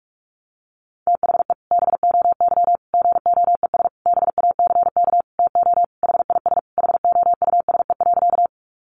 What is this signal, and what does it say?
Morse code, heard as a keyed tone, about 44 wpm, 700 Hz. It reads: THE BOY GOES BACK TO HIS HOUSE.
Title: Easy Phrases 45wpm